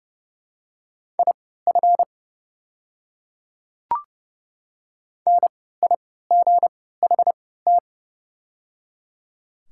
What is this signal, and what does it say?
Morse code, heard as a keyed tone, about 31 wpm, spaced as for 17 wpm, 700 Hz. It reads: IF NIGHT
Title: 3x Character Spacing - Top 300 Words - 30wpm